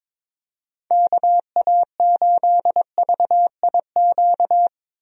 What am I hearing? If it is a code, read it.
Morse code, heard as a keyed tone, about 22 wpm, 700 Hz. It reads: KA8VIQ
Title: USA Call Signs - 6 Characters - 22wpm